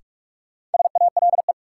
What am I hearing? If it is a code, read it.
Morse code, heard as a keyed tone, about 42 wpm, 700 Hz. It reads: SALE
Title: Sentences from Top 1400 Words - New Word Drill Alt - 45wpm